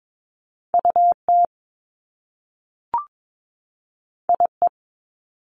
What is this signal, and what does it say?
Morse code, heard as a keyed tone, about 22 wpm, 700 Hz. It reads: UT IE
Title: Top Two Letter Combinations in Words 22wpm